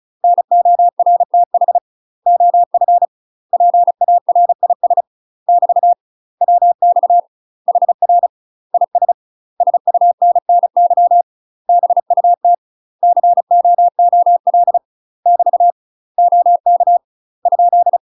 Morse code, heard as a keyed tone, about 34 wpm, 700 Hz. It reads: NORTH OF PARIS <BT> WX HR IS SUNNY BUT COOL <BT> OK ?
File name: UK MASTER QSOs TRAINING / 35WPM / UK MASTER QSOs 0024